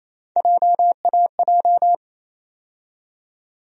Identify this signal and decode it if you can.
Morse code, heard as a keyed tone, about 28 wpm, 700 Hz. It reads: JAJ
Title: USA Call Sign Postfixes - 3 Letters 28wpm